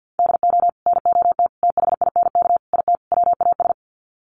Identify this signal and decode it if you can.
Morse code, heard as a keyed tone, about 50 wpm, 700 Hz. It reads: DO NOT THINK IT WAS
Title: Sets of 5 Words Encore 50wpm